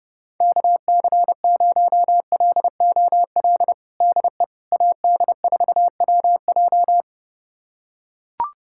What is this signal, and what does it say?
Morse code, heard as a keyed tone, about 30 wpm, 700 Hz. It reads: KC0LOL DE AD4WJ
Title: USA Callsign DE Callsign 30wpm